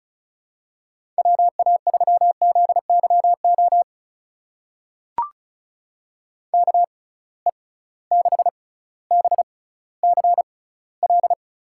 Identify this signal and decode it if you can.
Morse code, heard as a keyed tone, about 36 wpm, spaced as for 18 wpm, 700 Hz. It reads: WA3ZYO KE6BCL
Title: USA Call Signs - 6x Character Spacing - 35wpm